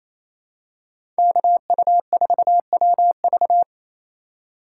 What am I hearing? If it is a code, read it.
Morse code, heard as a keyed tone, about 27 wpm, 700 Hz. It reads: KU4WV
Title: USA Call Signs - 5 Characters - 8x Character Spacing - 28wpm